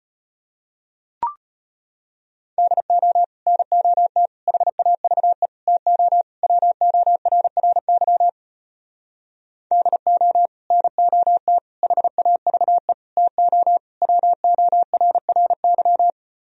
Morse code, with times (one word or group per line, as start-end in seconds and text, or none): 2.6-3.2 DO
3.5-4.3 NOT
4.5-5.5 HAVE
5.7-6.2 TO
6.4-8.3 WORRY
9.7-10.5 DO
10.7-11.6 NOT
11.8-12.9 HAVE
13.2-13.8 TO
14.0-16.1 WORRY